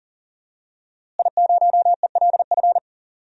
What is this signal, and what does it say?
Morse code, heard as a keyed tone, about 40 wpm, 700 Hz. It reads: I0ELF
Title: DXCC-100 Call Signs - 5 Characters - Repeated 2x - 40wpm